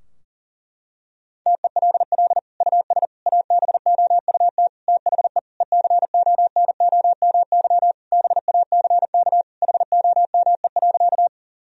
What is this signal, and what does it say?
Morse code, heard as a keyed tone, about 37 wpm, 700 Hz. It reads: TELL US ABOUT THE ECONOMY BACK HOME.
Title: Sentences from Top 1000 Words - Review Part 3 - 40wpm